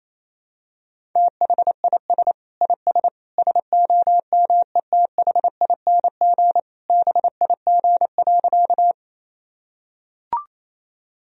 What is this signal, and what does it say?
Morse code, heard as a keyed tone, about 28 wpm, 700 Hz. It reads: THIS IS SOMETHING BIG.
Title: Sentences from Top 1000 Words - Review Part 1 - 28wpm